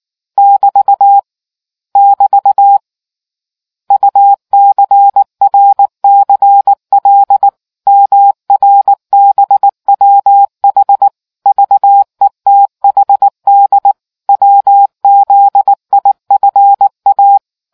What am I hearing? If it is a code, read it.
Morse code, heard as a keyed tone, about 19 wpm, 800 Hz. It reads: <BT> <BT> UCRCL MRBWH VETHD WZIFA